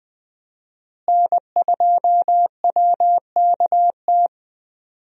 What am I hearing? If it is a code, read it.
Morse code, heard as a keyed tone, about 20 wpm, 700 Hz. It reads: N2WKT
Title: USA Call Signs - 5 Characters - Repeated 3x - 20wpm